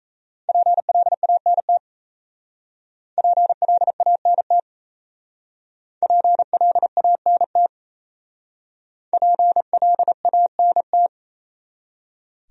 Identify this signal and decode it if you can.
Morse code, heard as a keyed tone, about 35 wpm, 700 Hz. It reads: PLANT PLANT PLANT PLANT
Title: Speed-Racing Top 300 Words 28wpm